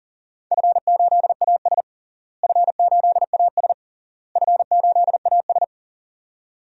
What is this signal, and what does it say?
Morse code, heard as a keyed tone, about 37 wpm, 700 Hz. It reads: F8AS F8AS F8AS
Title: DXCC-100 Call Signs - 4 Characters - Repeated 3x - 40wpm